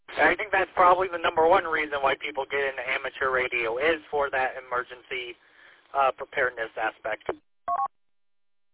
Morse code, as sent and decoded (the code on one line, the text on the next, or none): none